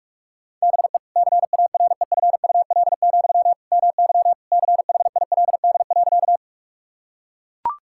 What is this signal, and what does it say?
Morse code, heard as a keyed tone, about 41 wpm, 700 Hz. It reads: BE CAREFUL, MY CHILD.
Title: Sentences from Top 1600 Words 45wpm